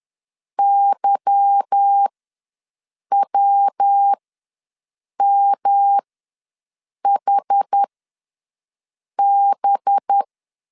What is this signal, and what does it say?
Morse code, heard as a keyed tone, about 10 wpm, spaced as for 6 wpm, 800 Hz. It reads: YWMHB
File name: LEZIONI / 9 2 ws 10wpm 4130